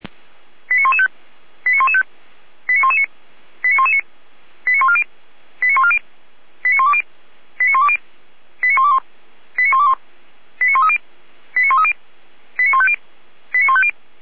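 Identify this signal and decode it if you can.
Five-tone selective call (ZVEI), heard as quick runs of five tones, tones of about 70 ms. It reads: 89115 89115 89119 89119 89140 89140 89120 89120 89121 89121 89130 89130 89150 89150
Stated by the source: Five-tone ZVEI selective call received on an SDR.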